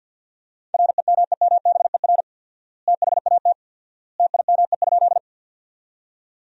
Morse code, heard as a keyed tone, about 49 wpm, 700 Hz. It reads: REMEMBER THAT TIME?
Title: 4x Word-Spacing - Sentences from Top 400 Words - 50wpm